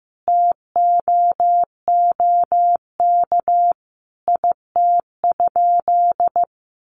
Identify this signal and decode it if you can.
Morse code, heard as a keyed tone, about 15 wpm, 700 Hz. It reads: TOOK IT?
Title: Sentences from Top 300 Words 15wpm